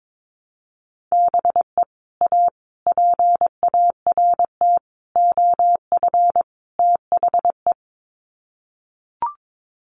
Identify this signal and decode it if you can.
Morse code, heard as a keyed tone, about 22 wpm, 700 Hz. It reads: BE A PART OF THE
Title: Sets of 5 Words Encore 22wpm